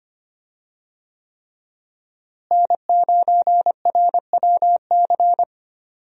Morse code, heard as a keyed tone, about 25 wpm, 700 Hz. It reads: N9RWC
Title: USA Call Signs - 5 Characters - Repeated 3x - 25wpm